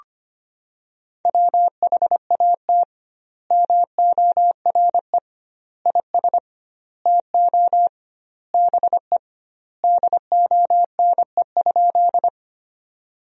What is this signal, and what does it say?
Morse code, heard as a keyed tone, about 25 wpm, 700 Hz. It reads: WHAT MORE IS TO BE DONE?
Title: 2x Word-Spacing - Sentences from Top 500 Words - 25wpm